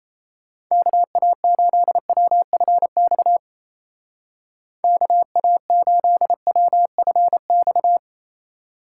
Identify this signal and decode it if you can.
Morse code, heard as a keyed tone, about 35 wpm, 700 Hz. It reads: KA8WFX KA8WFX
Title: Speed Racing - USA 2x3 Call Signs 28wpm